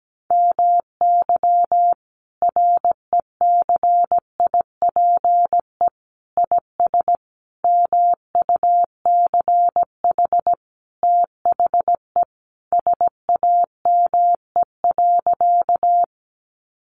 Morse code, heard as a keyed tone, about 17 wpm, 700 Hz. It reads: MY RECIPE IS MUCH THE SAME.
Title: Sentences from Top 1200 Words 17wpm